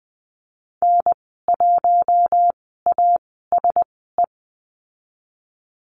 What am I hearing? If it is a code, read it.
Morse code, heard as a keyed tone, about 20 wpm, spaced as for 14 wpm, 700 Hz. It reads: N1ASE
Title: 076 Lesson 13g - Letter L - Call Signs 20wpm